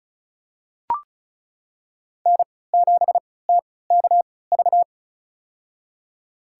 Morse code, heard as a keyed tone, about 34 wpm, spaced as for 20 wpm, 700 Hz. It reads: N7TKV
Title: USA Call Signs - 5 Characters - 3x Character Spacing - 35wpm